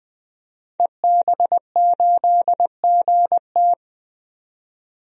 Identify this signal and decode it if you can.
Morse code, heard as a keyed tone, about 20 wpm, 700 Hz. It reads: EB8GT
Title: DXCC-100 Call Signs - 5 Characters - 20wpm